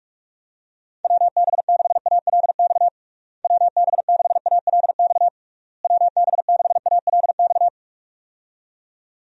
Morse code, heard as a keyed tone, about 42 wpm, 700 Hz. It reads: WB6ALX WB6ALX WB6ALX
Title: USA Call Signs - 6 Characters - Repeated 3x - 45wpm